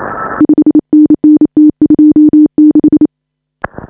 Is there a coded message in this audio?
Yes: Morse code from 0.4 to 3.1 s, heard as a keyed tone, about 28 wpm, 310 Hz. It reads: 5NN T26